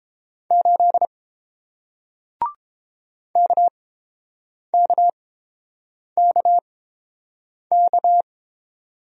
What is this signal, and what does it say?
Morse code, heard as a keyed tone, about 30 wpm, spaced as for 7 wpm, 700 Hz. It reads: 8 KKKK